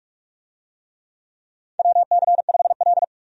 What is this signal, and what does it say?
Morse code, heard as a keyed tone, about 44 wpm, 700 Hz. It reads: WC5L
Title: USA Call Signs - 4 Characters - 45wpm